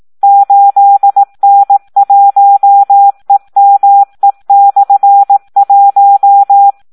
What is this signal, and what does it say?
Morse code, heard as a keyed tone, about 18 wpm, 800 Hz. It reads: 8N1EME/1
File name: Audio practica CW / SD 1 18